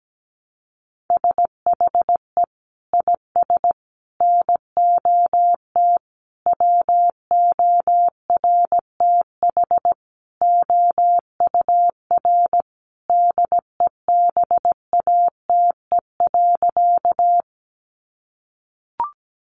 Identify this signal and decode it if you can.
Morse code, heard as a keyed tone, about 17 wpm, 700 Hz. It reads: SHE IS NOT WORTH OUR DEBATE.
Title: Sentences from Top 1100 Words 17wpm